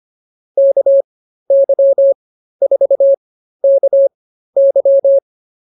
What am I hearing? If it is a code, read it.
Morse code, heard as a keyed tone, about 24 wpm, spaced as for 13 wpm, 550 Hz. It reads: KY4KY
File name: QSO 108 13